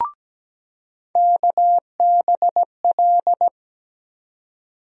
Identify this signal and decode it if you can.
Morse code, heard as a keyed tone, about 17 wpm, 700 Hz. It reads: KBL
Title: USA Call Sign Postfixes - 3 Letters 17wpm